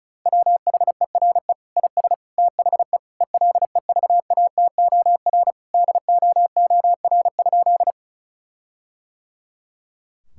Morse code, heard as a keyed tone, about 36 wpm, 700 Hz. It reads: WHERE IS THE ELEVATOR DOOR?
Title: Sentences from Top 1600 Words 35wpm